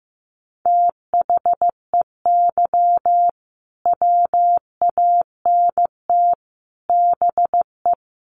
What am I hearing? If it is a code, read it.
Morse code, heard as a keyed tone, about 15 wpm, 700 Hz. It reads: THEY WANT BE
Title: Sentences from Top 1000 Words 15wpm